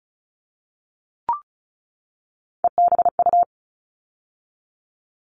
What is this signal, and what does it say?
Morse code, heard as a keyed tone, about 34 wpm, 700 Hz. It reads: EBU
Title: All Cut Numbers - 3-Digit 35wpm